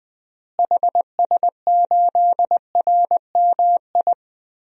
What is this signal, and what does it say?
Morse code, heard as a keyed tone, about 20 wpm, 700 Hz. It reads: HS8RMI
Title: DXCC-100 Call Signs - 6 Characters - Repeated 3x - 20wpm